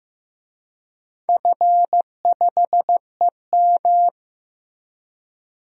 Morse code, heard as a keyed tone, about 15 wpm, 700 Hz. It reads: F5EM
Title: DXCC-100 Call Signs - 4 Characters - Repeated 3x - 15wpm